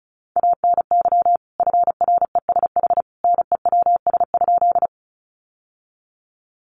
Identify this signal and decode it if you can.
Morse code, heard as a keyed tone, about 34 wpm, 700 Hz. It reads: ANY FRESH NEWS?
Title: Sentences from Top 1200 Words 35wpm